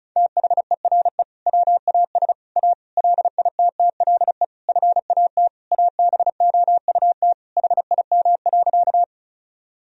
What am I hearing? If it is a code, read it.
Morse code, heard as a keyed tone, about 34 wpm, 700 Hz. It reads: THERE WAS A LITTLE FAT ABOUT HIM.
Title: Sentences from Top 1000 Words - Review Part 4 - 35wpm